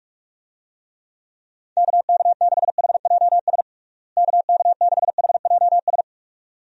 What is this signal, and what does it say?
Morse code, heard as a keyed tone, about 41 wpm, 700 Hz. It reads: KK6HJS KK6HJS
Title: USA Call Signs - 6 Characters - Repeated 2x - 45wpm